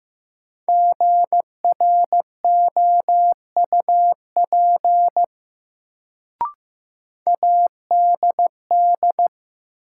Morse code, heard as a keyed tone, about 15 wpm, 700 Hz. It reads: GROUP ADD
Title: Sentences from Top 1000 Words - Review Part 1 - New Word Drill - 15wpm